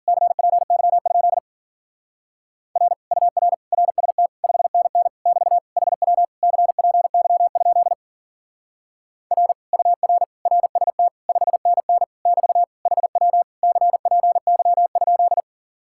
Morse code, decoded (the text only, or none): CPY? R UR RST 5NN <BT> HW CPY? R UR RST 5NN <BT> HW CPY?